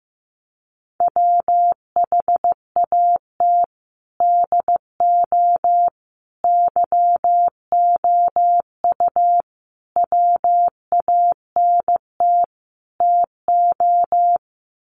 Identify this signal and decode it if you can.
Morse code, heard as a keyed tone, about 15 wpm, 700 Hz. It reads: WHAT DO YOU WANT TO
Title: Sets of 5 Words 15wpm